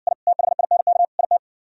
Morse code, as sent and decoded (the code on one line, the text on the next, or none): .. / - .... .. -. -.- / .. -
I THINK IT